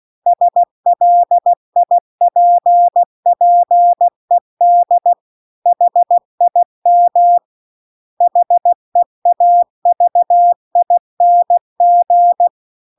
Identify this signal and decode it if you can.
Morse code, heard as a keyed tone, about 16 wpm, 700 Hz. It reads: SLIPPED HIM HEAVING